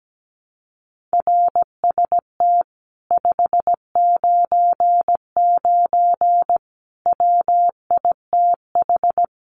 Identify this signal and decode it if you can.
Morse code, heard as a keyed tone, about 17 wpm, 700 Hz. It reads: RST 599 WITH
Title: Common QSO Phrases Encore 17wpm